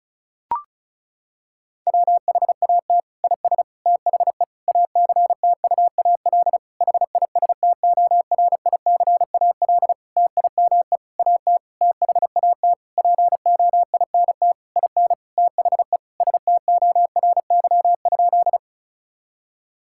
Morse code, heard as a keyed tone, about 34 wpm, 700 Hz. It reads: WHAT IS THE ACTUAL HISTORICAL TIME AT THAT POINT IN THE STORY?
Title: Sentences from Top 1000 Words - Review Part 5 - 35wpm